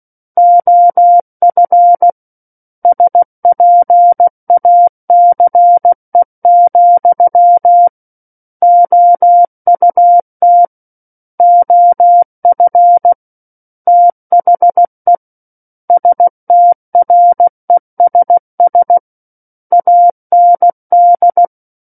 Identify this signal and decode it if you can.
Morse code, heard as a keyed tone, about 16 wpm, 700 Hz. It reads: OF SPACE, OUT OF THE STRESS AND
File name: Book One / 07-Chapter 7 - 16 WPM